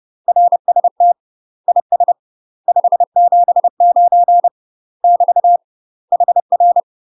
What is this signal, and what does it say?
Morse code, heard as a keyed tone, about 30 wpm, 700 Hz. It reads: RST IS 579 <BT> HR
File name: UK MASTER QSOs TRAINING / 30WPM / UK MASTER QSOs 0030